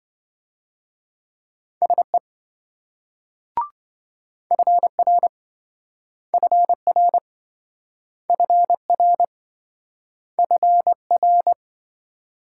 Morse code, heard as a keyed tone, about 27 wpm, 700 Hz. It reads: SE FR FR FR FR